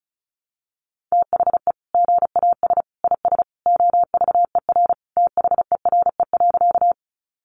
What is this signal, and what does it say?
Morse code, heard as a keyed tone, about 34 wpm, 700 Hz. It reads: THE GAS IS OVER THERE.
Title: Sentences from Top 1100 Words 35wpm